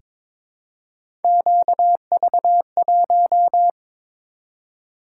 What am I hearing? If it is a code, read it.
Morse code, heard as a keyed tone, about 22 wpm, 700 Hz. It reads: QV1
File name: practice / 22wpm / Three Letter-Number Combinations - 22wpm